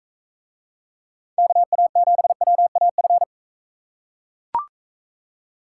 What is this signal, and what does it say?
Morse code, heard as a keyed tone, about 42 wpm, 700 Hz. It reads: KA7WAF